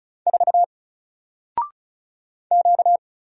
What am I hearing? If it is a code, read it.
Morse code, heard as a keyed tone, about 36 wpm, 700 Hz. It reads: 4 Q
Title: Single Character (Letter:Number:Punctuation:Prosign) Alt 35wpm